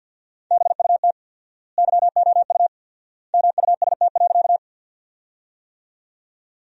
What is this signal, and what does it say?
Morse code, heard as a keyed tone, about 48 wpm, 700 Hz. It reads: BUT YOU MUST.
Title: 4x Word-Spacing - Sentences from Top 200 Words - 50wpm